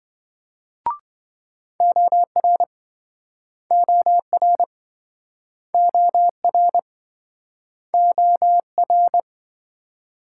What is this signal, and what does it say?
Morse code, heard as a keyed tone, about 26 wpm, 700 Hz. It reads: OR OR OR OR